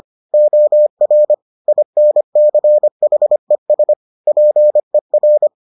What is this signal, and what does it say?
Morse code, heard as a keyed tone, about 25 wpm, 600 Hz. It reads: OR INCHES PER